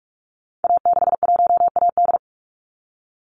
Morse code, heard as a keyed tone, about 44 wpm, 700 Hz. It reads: A61AD
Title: Call Signs Repeated 3x - 45wpm